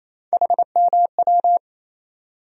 Morse code, heard as a keyed tone, about 28 wpm, 700 Hz. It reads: HMW
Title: USA Call Sign Postfixes - 3 Letters 28wpm